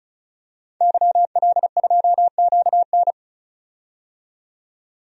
Morse code, heard as a keyed tone, about 36 wpm, 700 Hz. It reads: YL2QN
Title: DXCC-100 Call Signs - 5 Characters - 35wpm